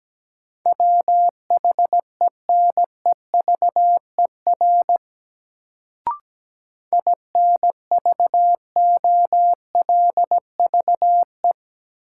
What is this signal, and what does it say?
Morse code, heard as a keyed tone, about 17 wpm, 700 Hz. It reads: WHENEVER INVOLVE
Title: Sentences from Top 1200 Words - New Word Drill - 17wpm